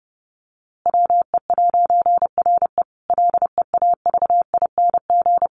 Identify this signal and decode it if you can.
Morse code, heard as a keyed tone, about 30 wpm, 700 Hz. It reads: WE'RE LEAVING
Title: A Sound Of Thunder 30wpm